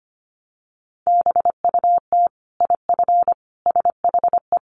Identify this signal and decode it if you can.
Morse code, heard as a keyed tone, about 25 wpm, 700 Hz. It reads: BUT IF SHE
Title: Sentences from Top 1500 Words 25wpm